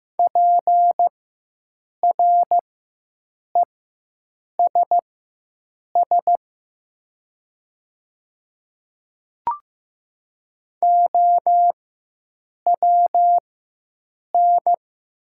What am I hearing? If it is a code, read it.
Morse code, heard as a keyed tone, about 15 wpm, spaced as for 7 wpm, 700 Hz. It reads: PRESS OWN
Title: 4x Character Spacing - Top 300 Words - 15wpm